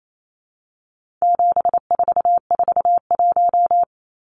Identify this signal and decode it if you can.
Morse code, heard as a keyed tone, about 28 wpm, 700 Hz. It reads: 7441
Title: Numbers - 4-Digit 28wpm